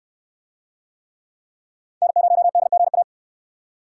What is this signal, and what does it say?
Morse code, heard as a keyed tone, about 68 wpm, 700 Hz. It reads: N0DGA